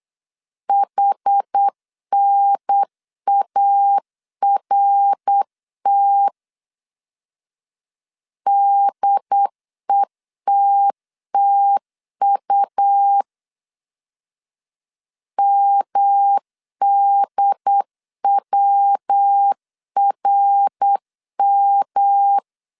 Morse code, heard as a keyed tone, about 8 wpm, 800 Hz. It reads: HNART DETTU MDWRM
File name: CW / DET / 1 3 1500C S B 8wpm 4141